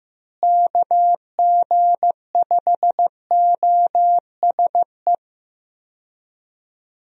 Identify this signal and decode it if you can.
Morse code, heard as a keyed tone, about 15 wpm, 700 Hz. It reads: KG5OSE
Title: USA Call Signs - 6 Characters - Repeated 2x - 15wpm